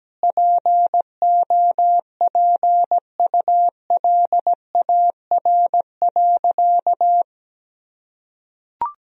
Morse code, heard as a keyed tone, about 17 wpm, 700 Hz. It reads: POPULAR.
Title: Sentences from Top 1000 Words - Review Part 3 - 17wpm